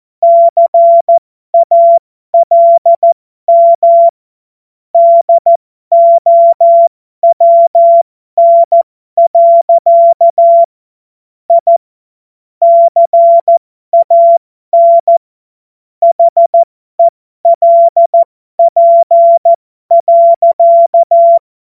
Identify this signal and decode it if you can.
Morse code, heard as a keyed tone, about 14 wpm, spaced as for 12 wpm, 680 Hz. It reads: CALM DOWN. I CAN HELP.
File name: Morsekurs / Witze